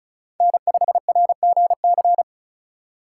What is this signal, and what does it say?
Morse code, heard as a keyed tone, about 34 wpm, 700 Hz. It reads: N5RGC
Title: USA Call Signs - 6x Character Spacing - 35wpm